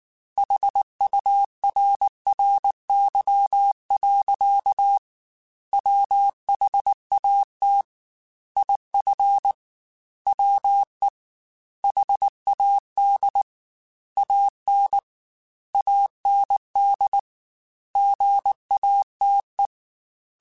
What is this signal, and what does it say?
Morse code, heard as a keyed tone, about 19 wpm, 800 Hz. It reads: HURRY. WHAT IF WE HAD AN AND GATE
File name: Technical Subjects / Technical Subjects-mp3s / AND Gate @18